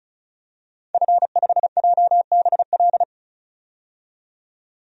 Morse code, heard as a keyed tone, about 34 wpm, 700 Hz. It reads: F5JBL
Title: DXCC-100 Call Signs - 5 Characters - 35wpm